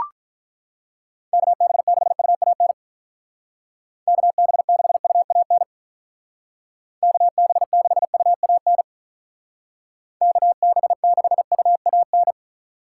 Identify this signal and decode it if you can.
Morse code, heard as a keyed tone, about 44 wpm, 700 Hz. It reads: KB6UAN KB6UAN KB6UAN KB6UAN
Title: Speed Racing - USA 2x3 Call Signs 35wpm